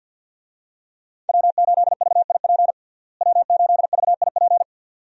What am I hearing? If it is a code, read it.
Morse code, heard as a keyed tone, about 50 wpm, 700 Hz. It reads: W8VIP W8VIP
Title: USA Call Signs - 5 Characters - Repeated 2x - 50wpm